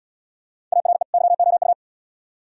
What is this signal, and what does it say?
Morse code, heard as a keyed tone, about 69 wpm, 700 Hz. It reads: ARE YOU